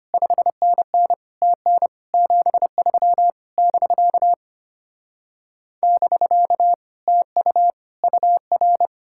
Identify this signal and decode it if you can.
Morse code, heard as a keyed tone, about 33 wpm, 700 Hz. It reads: 5NN TN 73 <BK> <BK> TU UR